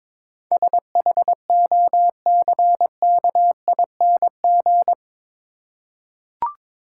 Morse code, heard as a keyed tone, about 21 wpm, 700 Hz. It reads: SHOCKING
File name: practice / 22wpm / Eight Letter Words - 22wpm